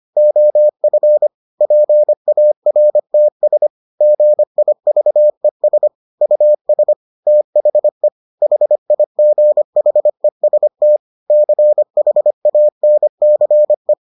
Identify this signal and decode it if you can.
Morse code, heard as a keyed tone, about 25 wpm, 600 Hz. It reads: OF PARTS GIVES US THE HIGHEST CHANCE